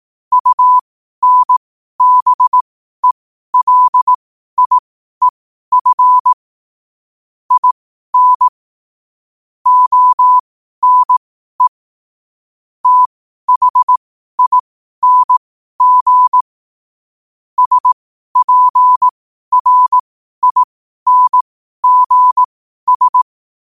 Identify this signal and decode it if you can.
Morse code, heard as a keyed tone, about 18 wpm, spaced as for 12 wpm, 1000 Hz. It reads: UNBELIEF IN ONE THING SPRINGS
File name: qotdfiles / 12wpm